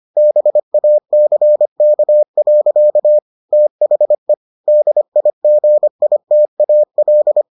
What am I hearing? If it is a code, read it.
Morse code, heard as a keyed tone, about 25 wpm, 600 Hz. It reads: BACK. THE DIGITAL